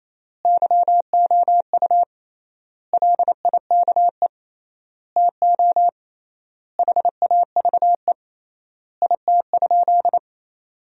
Morse code, heard as a keyed tone, about 28 wpm, 700 Hz. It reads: YOU LIKE TO HAVE IT?
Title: 3x Word-Spacing - Sentences from Top 100 Words - 28wpm